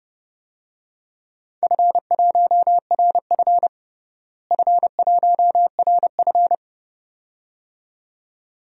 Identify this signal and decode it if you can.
Morse code, heard as a keyed tone, about 30 wpm, 700 Hz. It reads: F1RF F1RF
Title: DXCC-100 Call Signs - 4 Characters - Repeated 2x - 30wpm